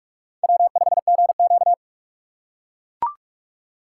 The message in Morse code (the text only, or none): W5GQ